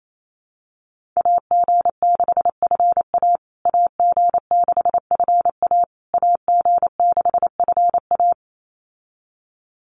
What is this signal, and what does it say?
Morse code, heard as a keyed tone, about 28 wpm, 700 Hz. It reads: AG6FA AG6FA AG6FA
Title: USA Call Signs Repeated 3x 28wpm